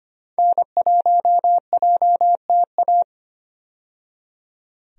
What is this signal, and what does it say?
Morse code, heard as a keyed tone, about 25 wpm, 700 Hz. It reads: N1JTA
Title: USA Call Signs - 5 Characters - 25wpm